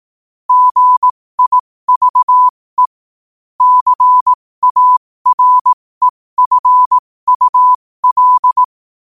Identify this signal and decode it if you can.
Morse code, heard as a keyed tone, about 18 wpm, spaced as for 15 wpm, 1000 Hz. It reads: GIVE CAREFUL
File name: qotdfiles / 15wpm